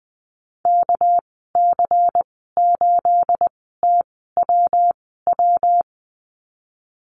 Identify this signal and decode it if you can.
Morse code, heard as a keyed tone, about 20 wpm, spaced as for 14 wpm, 700 Hz. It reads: KC8TWW
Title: 238 Lesson 38g - Letter q - Call Signs 20wpm